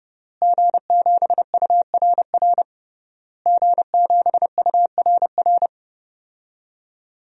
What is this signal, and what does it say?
Morse code, heard as a keyed tone, about 30 wpm, 700 Hz. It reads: G7URR G7URR
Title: DXCC-100 Call Signs - 5 Characters - Repeated 3x - 30wpm